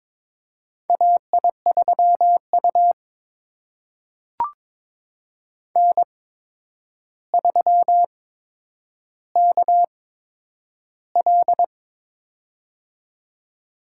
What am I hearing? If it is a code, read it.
Morse code, heard as a keyed tone, about 23 wpm, spaced as for 9 wpm, 700 Hz. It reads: AI3U N3KL
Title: USA Call Signs - 8x Character Spacing - 22wpm